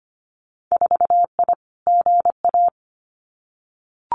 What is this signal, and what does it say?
Morse code, heard as a keyed tone, about 25 wpm, 700 Hz. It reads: 4I GA
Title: Winter Field Day Exchanges Encore 25wpm